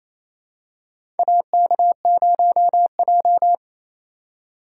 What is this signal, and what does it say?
Morse code, heard as a keyed tone, about 28 wpm, 700 Hz. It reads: AK0J
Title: USA Call Signs - 4 Characters - 4x Character Spacing - 28wpm